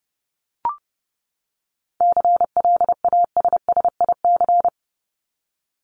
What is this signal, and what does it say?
Morse code, heard as a keyed tone, about 30 wpm, 700 Hz. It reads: CLASSIC
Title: Top 751-1000 Words 30wpm